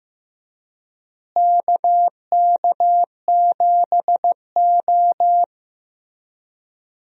Morse code, heard as a keyed tone, about 15 wpm, 700 Hz. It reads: KK7O
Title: USA Call Signs - 4 Characters - Repeated 3x - 15wpm